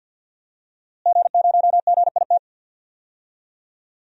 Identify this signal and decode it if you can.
Morse code, heard as a keyed tone, about 50 wpm, 700 Hz. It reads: G0GIT
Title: DXCC-100 Call Signs - 5 Characters - Repeated 2x - 50wpm